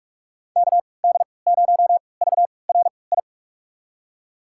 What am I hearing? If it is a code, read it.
Morse code, heard as a keyed tone, about 41 wpm, spaced as for 25 wpm, 700 Hz. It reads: KD0VRI
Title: USA Call Signs - 6 Characters - 3x Character Spacing - 45wpm